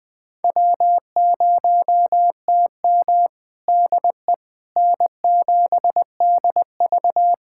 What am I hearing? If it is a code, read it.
Morse code, heard as a keyed tone, about 20 wpm, 700 Hz. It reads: W0TM DE N7DV
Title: USA Callsign DE Callsign - 4 Characters - 20wpm